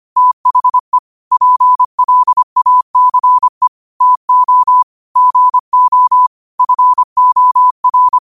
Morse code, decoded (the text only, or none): THE PLACE TO GO FOR